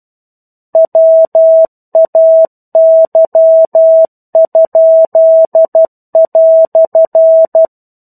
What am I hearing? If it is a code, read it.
Morse code, heard as a keyed tone, about 12 wpm, 650 Hz. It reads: WAY?"
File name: 01-Chapter-01 12 WPM